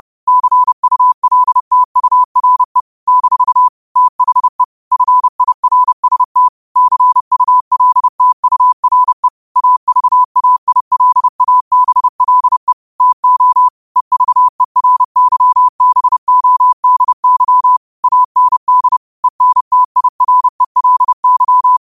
Morse code, heard as a keyed tone, about 30 wpm, 1000 Hz. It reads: CULTURE - THE FIRST CULTURE AVAILABLE TO EVERYBODY AND ENTIRELY